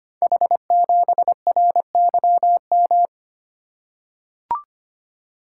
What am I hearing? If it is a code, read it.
Morse code, heard as a keyed tone, about 26 wpm, 700 Hz. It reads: H7RYM
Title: Code Groups - 5 Letter:Numbers - 25wpm